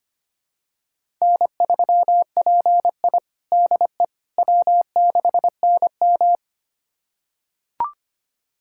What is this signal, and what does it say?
Morse code, heard as a keyed tone, about 25 wpm, 700 Hz. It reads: N3PI DE W6NM
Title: USA Callsign DE Callsign - 4 Characters - 25wpm